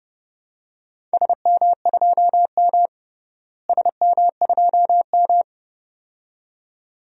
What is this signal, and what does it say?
Morse code, heard as a keyed tone, about 29 wpm, 700 Hz. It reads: SM2M SM2M
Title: DXCC-100 Call Signs - 4 Characters - Repeated 2x - 30wpm